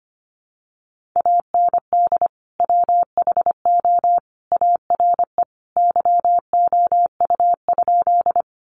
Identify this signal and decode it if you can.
Morse code, heard as a keyed tone, about 25 wpm, 700 Hz. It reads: AND WHO ARE YOU?
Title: Sentences from Top 100 Words 25wpm